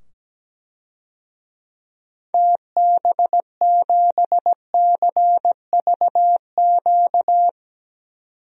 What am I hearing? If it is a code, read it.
Morse code, heard as a keyed tone, about 17 wpm, 700 Hz. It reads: TB7CVQ
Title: DXCC-100 Call Signs - 6 Characters - 17wpm